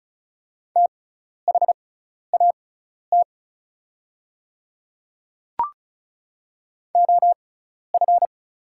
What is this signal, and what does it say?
Morse code, heard as a keyed tone, about 34 wpm, spaced as for 12 wpm, 700 Hz. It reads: THAT OF
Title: 6x Character Spacing - Top 100 Words - 35wpm